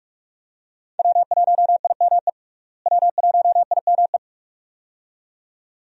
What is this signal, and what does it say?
Morse code, heard as a keyed tone, about 44 wpm, 700 Hz. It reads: W1IME W1IME
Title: USA Call Signs - 5 Characters - Repeated 2x - 45wpm